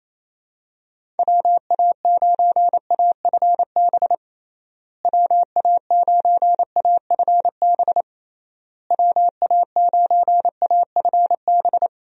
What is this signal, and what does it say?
Morse code, heard as a keyed tone, about 27 wpm, 700 Hz. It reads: WA9AFB WA9AFB WA9AFB
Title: USA Call Signs - 6 Characters - Repeated 3x - 28wpm